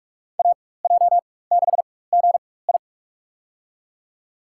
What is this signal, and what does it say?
Morse code, heard as a keyed tone, about 46 wpm, spaced as for 21 wpm, 690 Hz. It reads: AJ6GI
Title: USA Call Signs - 5 Characters - 4x Character Spacing - 45wpm